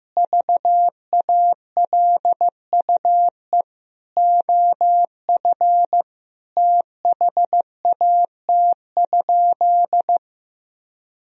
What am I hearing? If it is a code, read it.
Morse code, heard as a keyed tone, about 15 wpm, 700 Hz. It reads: VALUE OF THAT?
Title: Sentences from Top 1000 Words - Review Part 3 - 15wpm